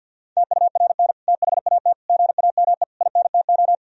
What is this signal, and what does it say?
Morse code, heard as a keyed tone, about 50 wpm, 700 Hz. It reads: TURN THAT GAME INTO
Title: Sentences from Top 1000 Words - Review Part 3 - 50wpm